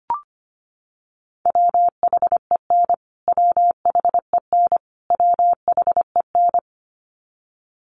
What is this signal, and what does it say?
Morse code, heard as a keyed tone, about 25 wpm, 700 Hz. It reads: WHEN WHEN WHEN
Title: Top 300 Words Repeated 25wpm